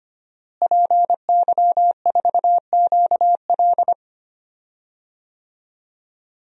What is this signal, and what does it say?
Morse code, heard as a keyed tone, about 25 wpm, 700 Hz. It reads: PY4QL
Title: DXCC-100 Call Signs - 5 Characters - Repeated 2x - 25wpm